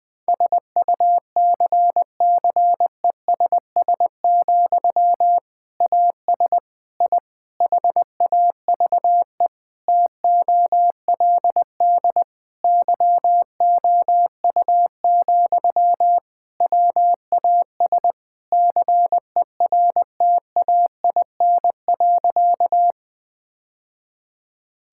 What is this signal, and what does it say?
Morse code, heard as a keyed tone, about 20 wpm, 700 Hz. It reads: SUCCESS, AS I HAVE TOLD YOU, WAS CERTAIN.
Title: Sentences from Top 1000 Words - Review Part 6 - 20wpm